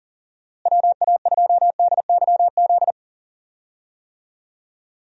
Morse code, heard as a keyed tone, about 40 wpm, 700 Hz. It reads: WA2DYZ
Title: USA Call Signs - 6 Characters - Repeated 3x - 40wpm